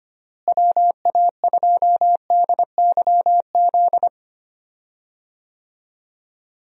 Morse code, heard as a keyed tone, about 25 wpm, 700 Hz. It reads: WA2DYZ
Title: USA Call Signs - 6 Characters - Repeated 3x - 25wpm